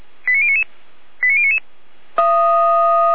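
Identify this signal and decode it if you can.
Five-tone selective call (ZVEI), heard as quick runs of five tones, tones of about 70 ms. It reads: 89099 89099
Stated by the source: ZVEI five-tone selective call received on an SDR.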